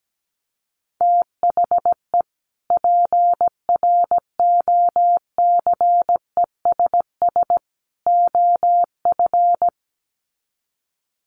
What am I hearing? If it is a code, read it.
Morse code, heard as a keyed tone, about 17 wpm, 700 Hz. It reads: THE PROCESS OF
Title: Sets of 3 Words 17wpm